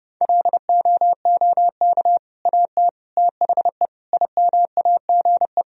Morse code, heard as a keyed tone, about 30 wpm, 700 Hz. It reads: LOOK AT THE IMAGE
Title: Sentences from Top 1000 Words - Review Part 4 - 30wpm